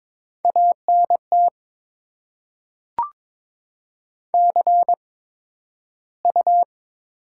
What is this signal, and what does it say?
Morse code, heard as a keyed tone, about 22 wpm, spaced as for 12 wpm, 700 Hz. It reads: ANT CU
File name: Top 20 QSO Elements - 8x Character Spacing - 22wpm